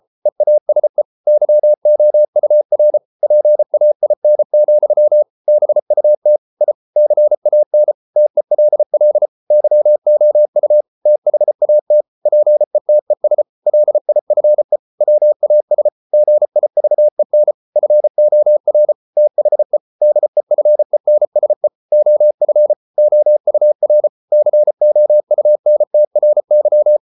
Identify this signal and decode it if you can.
Morse code, heard as a keyed tone, about 33 wpm, 600 Hz. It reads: EASE YOUR PAIN, BUT I CAN TELL YOU THAT PETES LIFE WAS GIVEN FOR THE DEFENSE OF OUR COUNTRY